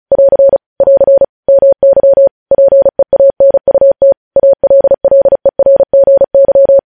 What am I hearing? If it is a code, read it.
Morse code, heard as a keyed tone, about 36 wpm, 550 Hz. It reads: <AR> <AR> MY PEANUT ALLERGY
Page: news headlines in morse code at 35 wpm